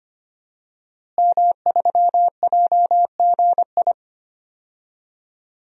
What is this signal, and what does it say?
Morse code, heard as a keyed tone, about 25 wpm, 700 Hz. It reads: M3JGI
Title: DXCC-100 Call Signs - 5 Characters - Repeated 2x - 25wpm